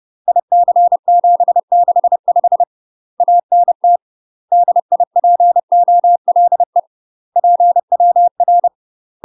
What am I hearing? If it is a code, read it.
Morse code, heard as a keyed tone, about 30 wpm, 700 Hz. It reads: IC765 ANT DIPOLE PWR
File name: UK MASTER QSOs TRAINING / 30WPM / UK MASTER QSOs 0038